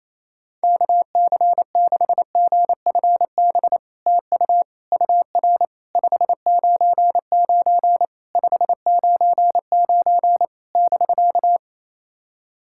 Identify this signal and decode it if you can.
Morse code, heard as a keyed tone, about 28 wpm, 700 Hz. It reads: KC6GFB TU UR 599 599 <BK>
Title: SOTA QSO Practice 28wpm